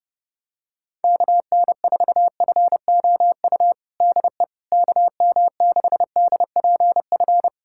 Morse code, heard as a keyed tone, about 28 wpm, 700 Hz. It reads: KN4FOU DE KM6DPF
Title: USA Callsign DE Callsign - 6 Characters - 30wpm